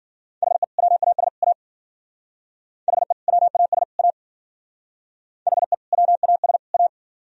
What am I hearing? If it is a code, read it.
Morse code, heard as a keyed tone, about 52 wpm, 700 Hz. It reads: HE WAS A HE WAS A HE WAS A